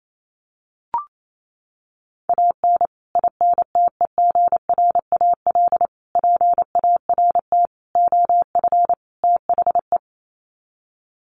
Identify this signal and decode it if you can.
Morse code, heard as a keyed tone, about 28 wpm, 700 Hz. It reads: AN INTEGRAL PART OF THE
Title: Sets of 5 Words 28wpm